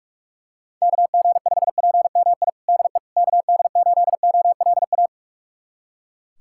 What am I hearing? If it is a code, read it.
Morse code, heard as a keyed tone, about 42 wpm, 700 Hz. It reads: KG5PMI DE KD8OLA
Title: USA Callsign DE Callsign - 6 Characters - 45wpm